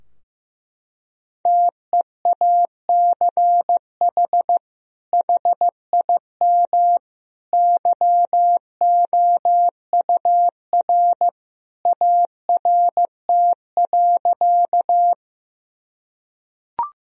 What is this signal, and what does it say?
Morse code, heard as a keyed tone, about 15 wpm, 700 Hz. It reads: TEACH HIM YOUR ART.
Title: Sentences from Top 1000 Words - Review Part 3 - 15wpm